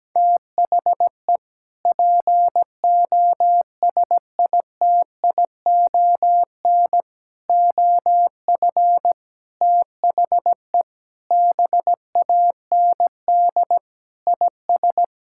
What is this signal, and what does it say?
Morse code, heard as a keyed tone, about 17 wpm, 700 Hz. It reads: THE POSITION OF THE BAND IS
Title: Sentences from Top 1600 Words 17wpm